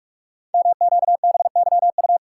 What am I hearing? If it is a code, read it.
Morse code, heard as a keyed tone, about 44 wpm, 690 Hz. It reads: MQBYU
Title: Code Groups - 5 Letters - 45wpm